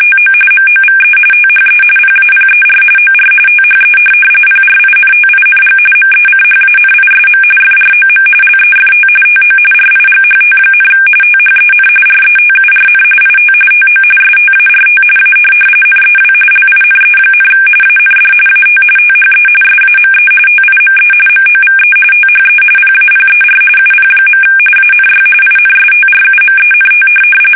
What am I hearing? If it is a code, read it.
RTTY, about 45.45 baud. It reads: TO MOVE THE PAPER OR THE PRINTING MECHANISM FROM LEFT TO RIGHT, THEN RETURN IT TO START A NEW LINE, AND AT THE SAME TIME MOVE THE PAPER UP. ANY INSTRUMENT
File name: fsk850